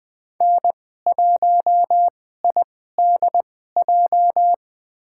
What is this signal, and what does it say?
Morse code, heard as a keyed tone, about 20 wpm, spaced as for 14 wpm, 700 Hz. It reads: N1IDJ
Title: USA Call Signs - 5 Characters - 2x Character Spacing - 20wpm